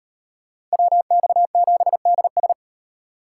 Morse code, heard as a keyed tone, about 36 wpm, 700 Hz. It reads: WX7DS